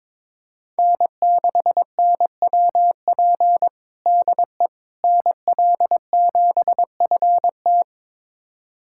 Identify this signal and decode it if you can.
Morse code, heard as a keyed tone, about 22 wpm, 700 Hz. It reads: N6NWP DE NL7FT
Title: USA Callsign DE Callsign - 5 Characters - 22wpm